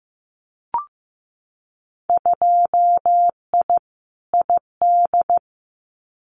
Morse code, heard as a keyed tone, about 15 wpm, 700 Hz. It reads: 2I ID
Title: Winter Field Day Exchanges Encore 15wpm